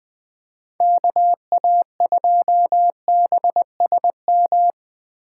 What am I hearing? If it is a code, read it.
Morse code, heard as a keyed tone, about 20 wpm, 700 Hz. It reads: KA2BSM
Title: USA Call Signs - 6 Characters - 3x Character Spacing - 20wpm